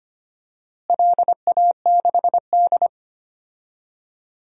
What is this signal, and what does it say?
Morse code, heard as a keyed tone, about 25 wpm, 700 Hz. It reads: LA6D
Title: DXCC-100 Call Signs - 4 Characters - Repeated 3x - 25wpm